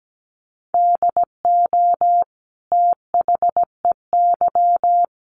Morse code, heard as a keyed tone, about 17 wpm, 700 Hz. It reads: DO THEY
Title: Sentences from Top 100 Words 17wpm